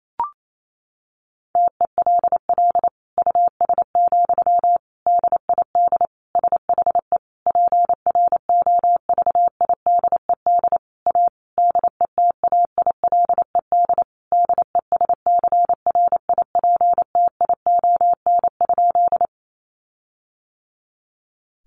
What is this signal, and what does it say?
Morse code, heard as a keyed tone, about 28 wpm, 700 Hz. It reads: TELL US, DID SHE PROVIDED A DETAILED DESCRIPTION?
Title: Sentences from Top 1500 Words 28wpm